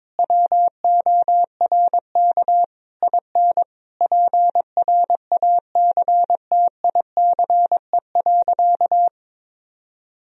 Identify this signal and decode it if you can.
Morse code, heard as a keyed tone, about 22 wpm, 700 Hz. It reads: WORK IN PRACTICE.
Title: Sentences from Top 1000 Words - Review Part 4 - 22wpm